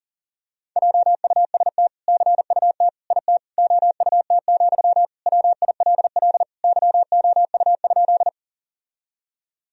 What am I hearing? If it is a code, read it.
Morse code, heard as a keyed tone, about 40 wpm, 700 Hz. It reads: JUST CUT IT OUT, WILL YOU?
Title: Sentences from Top 1000 Words - Review Part 1 - 40wpm